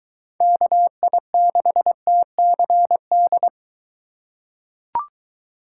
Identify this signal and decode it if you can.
Morse code, heard as a keyed tone, about 22 wpm, 700 Hz. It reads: KI6TCD